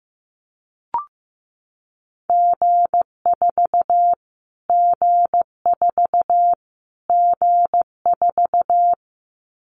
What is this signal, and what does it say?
Morse code, heard as a keyed tone, about 15 wpm, 700 Hz. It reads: G4 G4 G4
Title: Callsign Prefixes for Top 25 DXCC - Repeated 3x - 15wpm